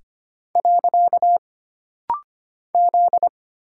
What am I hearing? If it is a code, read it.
Morse code, heard as a keyed tone, about 25 wpm, 700 Hz. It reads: . Z
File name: Single Character (Letter:Number:Punctuation:Prosign) - Rapid-Fire - 25wpm